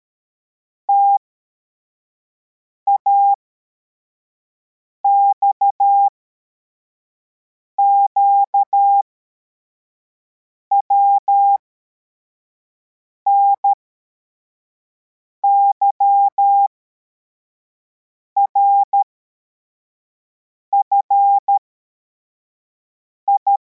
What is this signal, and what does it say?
Morse code, heard as a keyed tone, about 13 wpm, spaced as for 4 wpm, 800 Hz. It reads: TAXQW NYRFI